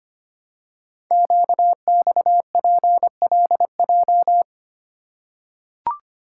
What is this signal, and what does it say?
Morse code, heard as a keyed tone, about 26 wpm, 700 Hz. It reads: QXPLJ
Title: Code Groups - 5 Letters - 25wpm